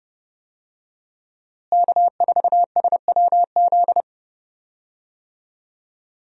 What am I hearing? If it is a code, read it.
Morse code, heard as a keyed tone, about 31 wpm, 700 Hz. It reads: K4SWZ
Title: USA Call Signs - 5 Characters - Repeated 2x - 30wpm